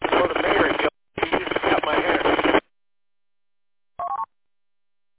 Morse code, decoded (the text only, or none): none